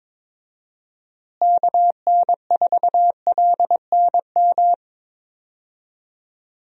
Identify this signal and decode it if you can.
Morse code, heard as a keyed tone, about 21 wpm, 700 Hz. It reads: KN4LNM
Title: USA Call Signs - 6 Characters - Repeated 2x - 22wpm